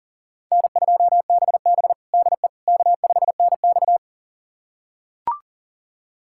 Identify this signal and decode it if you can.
Morse code, heard as a keyed tone, about 40 wpm, 700 Hz. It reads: N2BB DE K5NX
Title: USA Callsign DE Callsign - 4 Characters - 40wpm